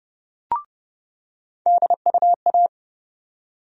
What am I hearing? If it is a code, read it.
Morse code, heard as a keyed tone, about 32 wpm, 700 Hz. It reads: DUA
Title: USA Call Sign Postfixes - 3 Letters 30wpm